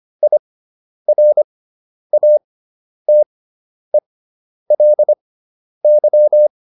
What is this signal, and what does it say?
Morse code, heard as a keyed tone, about 25 wpm, spaced as for 10 wpm, 600 Hz. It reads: IRATELY